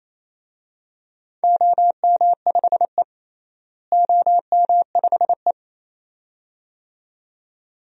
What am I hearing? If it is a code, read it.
Morse code, heard as a keyed tone, about 28 wpm, 700 Hz. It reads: OM5E OM5E
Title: DXCC-100 Call Signs - 4 Characters - Repeated 2x - 28wpm